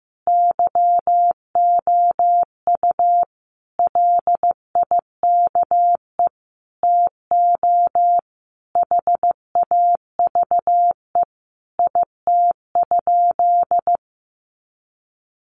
Morse code, heard as a keyed tone, about 15 wpm, 700 Hz. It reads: YOU LIKE TO HAVE IT?
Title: Sentences from Top 100 Words 15wpm